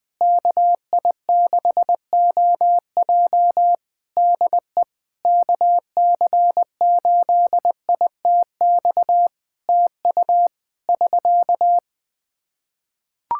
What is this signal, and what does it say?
Morse code, heard as a keyed tone, about 21 wpm, 700 Hz. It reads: KI6OJ DE KC8ITX TU <SK>
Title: K1USN SST QSO Practice 20wpm